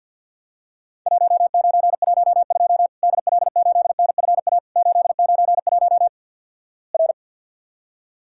Morse code, heard as a keyed tone, about 51 wpm, 700 Hz. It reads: 1912 DL8NFA 891 R
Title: Worked All Europe DX Contest - QTC Exchange - 50wpm